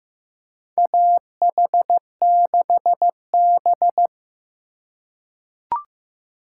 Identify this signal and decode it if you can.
Morse code, heard as a keyed tone, about 15 wpm, 700 Hz. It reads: AH6B
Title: USA Call Signs - 4 Characters - 2x Character Spacing - 15wpm